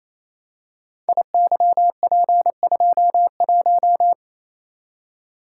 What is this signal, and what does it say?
Morse code, heard as a keyed tone, about 28 wpm, 700 Hz. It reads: IYP21
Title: Code Groups - 5 Letter:Numbers - 28wpm